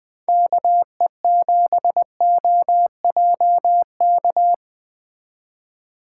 Morse code, heard as a keyed tone, about 20 wpm, 700 Hz. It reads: KE7OJK
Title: Speed Racing - USA 2x3 Call Signs 17wpm